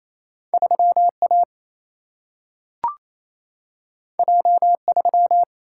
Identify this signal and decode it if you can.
Morse code, heard as a keyed tone, about 28 wpm, 700 Hz. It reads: 3A J3
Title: Single-Letter Single-Number 28wpm